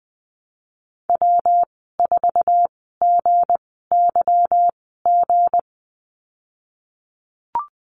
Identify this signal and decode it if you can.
Morse code, heard as a keyed tone, about 20 wpm, spaced as for 15 wpm, 700 Hz. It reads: W4GYG
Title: 186 Lesson 30g - Letter b - Call Signs 20wpm